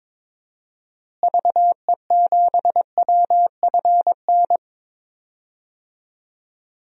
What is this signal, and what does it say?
Morse code, heard as a keyed tone, about 22 wpm, 700 Hz. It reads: VE7WFN
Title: DXCC-100 Call Signs - 6 Characters - Repeated 2x - 22wpm